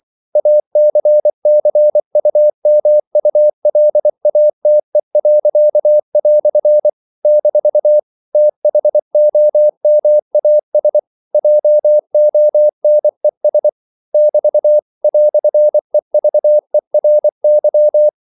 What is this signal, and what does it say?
Morse code, heard as a keyed tone, about 24 wpm, 600 Hz. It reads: ACCUMULATE." - THOMAS JONES <BT> "EVERY
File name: CW / Practice / EnglishQuotations0324